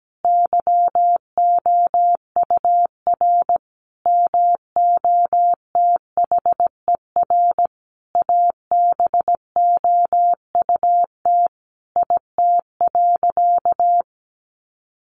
Morse code, heard as a keyed tone, about 17 wpm, 700 Hz. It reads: YOUR MOTHER ABOUT IT.
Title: Sentences from Top 1100 Words 17wpm